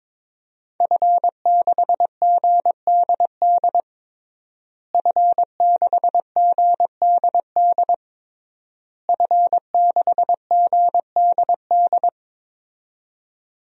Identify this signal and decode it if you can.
Morse code, heard as a keyed tone, about 22 wpm, 700 Hz. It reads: F6GDD F6GDD F6GDD
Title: DXCC-100 Call Signs - 5 Characters - Repeated 3x - 22wpm